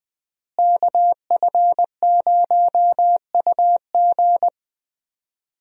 Morse code, heard as a keyed tone, about 20 wpm, 700 Hz. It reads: KF0UG
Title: USA Call Signs - 5 Characters - Repeated 3x - 20wpm